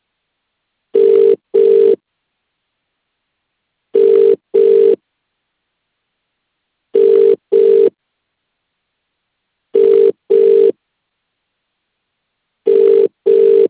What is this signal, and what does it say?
Ringback tone (UK), 400 + 450 Hz.